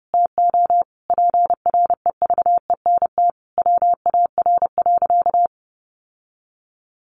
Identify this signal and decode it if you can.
Morse code, heard as a keyed tone, about 30 wpm, 700 Hz. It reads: TO PREVENT WAR.
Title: Sentences from Top 1100 Words 30wpm